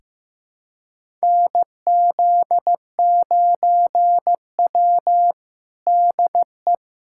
Morse code, heard as a keyed tone, about 15 wpm, 700 Hz. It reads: NZ9W DE
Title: USA Callsign DE Callsign - 4 Characters - 15wpm